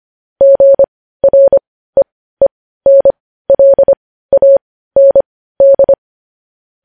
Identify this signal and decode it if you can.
Morse code, heard as a keyed tone, about 25 wpm, spaced as for 15 wpm, 550 Hz. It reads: GREENLAND
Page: news headlines in morse code at 15 wpm